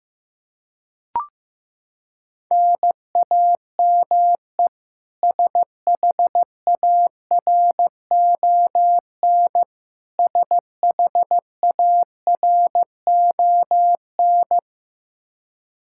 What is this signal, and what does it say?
Morse code, heard as a keyed tone, about 16 wpm, 700 Hz. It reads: NAME SHARON SHARON
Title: Practice QSO - Exchange 1 - Beginning - 15wpm